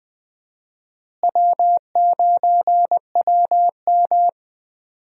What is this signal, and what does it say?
Morse code, heard as a keyed tone, about 19 wpm, 700 Hz. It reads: W9WM
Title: USA Call Signs - 4 Characters - Repeated 3x - 20wpm